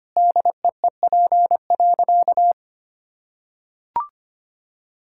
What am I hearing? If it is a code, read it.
Morse code, heard as a keyed tone, about 25 wpm, 700 Hz. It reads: DEEP.
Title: Sentences from Top 1000 Words - Review Part 6 - 25wpm